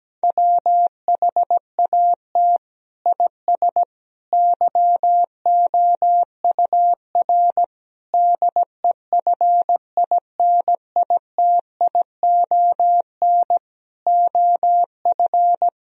Morse code, heard as a keyed tone, about 17 wpm, 700 Hz. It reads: WHAT IS YOUR DEFINITION OF
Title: Sentences from Top 1000 Words - Review Part 4 - 17wpm